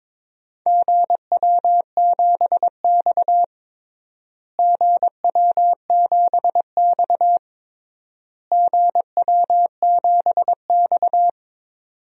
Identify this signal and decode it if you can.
Morse code, heard as a keyed tone, about 22 wpm, 700 Hz. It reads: GW7X GW7X GW7X
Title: DXCC-100 Call Signs - 4 Characters - Repeated 3x - 22wpm